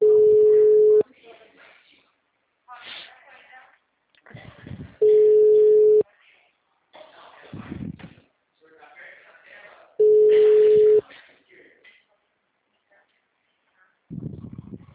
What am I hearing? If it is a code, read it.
Ringback tone (Europe, 425 Hz), with speech.